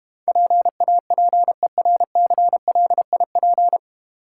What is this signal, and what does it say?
Morse code, heard as a keyed tone, about 32 wpm, 700 Hz. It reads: PAPERCLIP